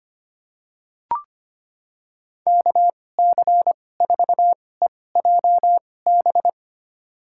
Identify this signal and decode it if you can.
Morse code, heard as a keyed tone, about 25 wpm, spaced as for 18 wpm, 700 Hz. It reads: KC4EJB
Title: USA Call Signs - 2x Character Spacing - 25wpm